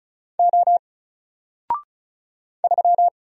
Morse code, heard as a keyed tone, about 36 wpm, 700 Hz. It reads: O 3
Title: Single Letter-Number Alt 35wpm